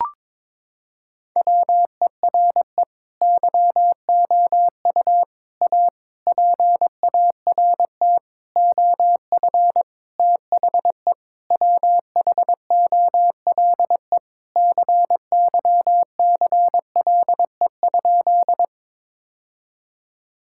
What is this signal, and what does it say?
Morse code, heard as a keyed tone, about 22 wpm, 700 Hz. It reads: WERE YOU A PART OF THE WHOLE CYCLE?
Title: Sentences from Top 1000 Words - Review Part 6 - 22wpm